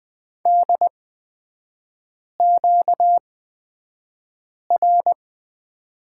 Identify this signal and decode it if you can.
Morse code, heard as a keyed tone, about 20 wpm, spaced as for 5 wpm, 700 Hz. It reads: DQR
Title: 236 Lesson 38e - Letter q - Three Characters 20wpm